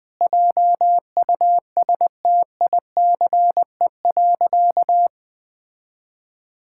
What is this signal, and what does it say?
Morse code, heard as a keyed tone, about 20 wpm, 700 Hz. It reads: JUSTICE.
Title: Sentences with 4-Word Phrases - 20wpm